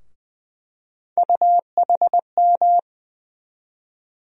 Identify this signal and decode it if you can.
Morse code, heard as a keyed tone, about 20 wpm, 700 Hz. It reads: UHM
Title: USA Call Sign Postfixes - 3 Letters 20wpm